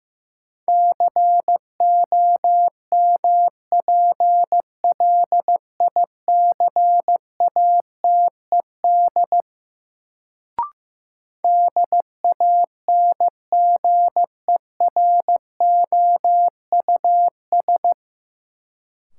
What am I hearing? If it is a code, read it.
Morse code, heard as a keyed tone, about 15 wpm, 700 Hz. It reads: COMPLICATED DANGEROUS
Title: Sentences from Top 1200 Words - New Word Drill - 15wpm